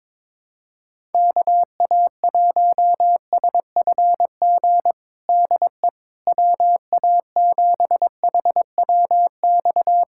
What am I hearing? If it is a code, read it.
Morse code, heard as a keyed tone, about 22 wpm, 700 Hz. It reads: KA1SFG DE WA7HWX
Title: USA Callsign DE Callsign 22wpm